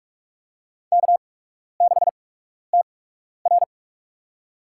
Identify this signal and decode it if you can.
Morse code, heard as a keyed tone, about 46 wpm, spaced as for 12 wpm, 700 Hz. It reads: K6TR
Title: USA Call Signs - 4 Characters - 8x Character Spacing - 45wpm